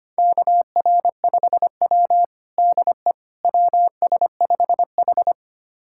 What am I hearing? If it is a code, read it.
Morse code, heard as a keyed tone, about 25 wpm, 700 Hz. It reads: KR5W DE WS5H
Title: USA Callsign DE Callsign - 4 Characters - 25wpm